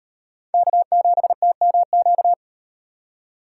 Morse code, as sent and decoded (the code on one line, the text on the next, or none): -.- --... - -- --.-
K7TMQ